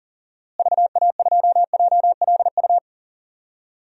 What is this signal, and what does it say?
Morse code, heard as a keyed tone, about 40 wpm, 700 Hz. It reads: VA2JLU
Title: DXCC-100 Call Signs - 6 Characters - Repeated 3x - 40wpm